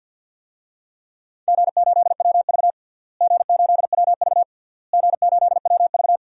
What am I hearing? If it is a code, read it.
Morse code, heard as a keyed tone, about 48 wpm, 700 Hz. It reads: G8WV G8WV G8WV
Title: DXCC-100 Call Signs - 4 Characters - Repeated 3x - 50wpm